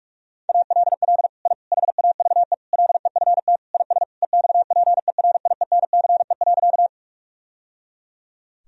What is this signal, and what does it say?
Morse code, heard as a keyed tone, about 46 wpm, 700 Hz. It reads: ALL I HAVE LEFT IS EXPERIENCE.
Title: Sentences from Top 1000 Words - Review Part 3 - 45wpm